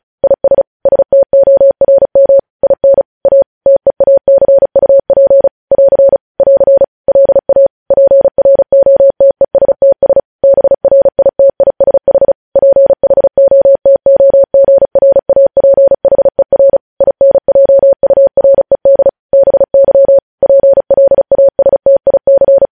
Morse code, heard as a keyed tone, about 36 wpm, 550 Hz. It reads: IS STORM IN A TEACUP <AR> <AR> LA PROTESTS BRITISH PHOTOGRAPHER INJURED BY PLASTIC
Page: news headlines in morse code at 35 wpm